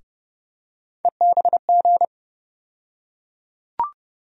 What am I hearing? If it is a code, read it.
Morse code, heard as a keyed tone, about 30 wpm, 700 Hz. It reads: EBG
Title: USA Call Sign Postfixes - 3 Letters 30wpm